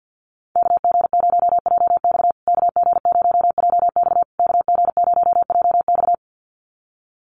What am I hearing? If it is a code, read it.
Morse code, heard as a keyed tone, about 48 wpm, 700 Hz. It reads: KG0JX KG0JX KG0JX
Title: USA Call Signs Repeated 3x 50wpm